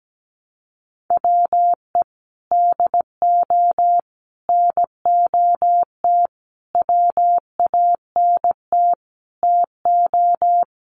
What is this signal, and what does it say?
Morse code, heard as a keyed tone, about 17 wpm, 700 Hz. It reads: WE DO NOT WANT TO
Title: Sets of 5 Words 17wpm